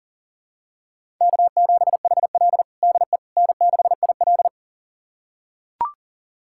Morse code, heard as a keyed tone, about 40 wpm, 700 Hz. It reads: K7HL DE N6IL
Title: USA Callsign DE Callsign - 4 Characters - 40wpm